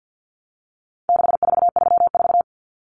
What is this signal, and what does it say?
Morse code, heard as a keyed tone, about 48 wpm, 700 Hz. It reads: 6434
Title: Numbers - 4-Digit 50wpm